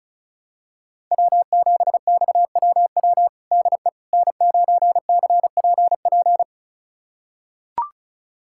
Morse code, heard as a keyed tone, about 34 wpm, 700 Hz. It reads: W7XWW DE N9CPP
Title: USA Callsign DE Callsign - 5 Characters - 35wpm